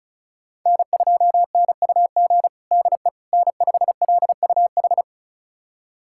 Morse code, heard as a keyed tone, about 34 wpm, 700 Hz. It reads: N2NUG DE N5LUH
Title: USA Callsign DE Callsign - 5 Characters - 35wpm